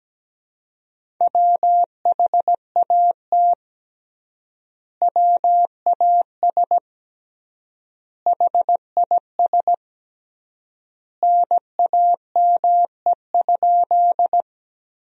Morse code, heard as a keyed tone, about 17 wpm, 700 Hz. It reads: WHAT WAS HIS NAME?
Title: 3x Word-Spacing - Sentences from Top 200 Words - 17wpm